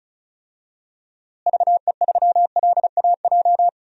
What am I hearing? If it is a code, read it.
Morse code, heard as a keyed tone, about 36 wpm, 700 Hz. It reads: VE3LAJ
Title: DXCC-100 Call Signs - 6 Characters - Repeated 2x - 35wpm